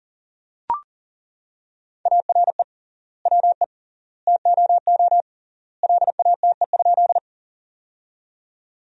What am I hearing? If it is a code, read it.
Morse code, heard as a keyed tone, about 40 wpm, 700 Hz. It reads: ARE WE TOO LATE?
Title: 3x Word-Spacing - Sentences from Top 300 Words - 40wpm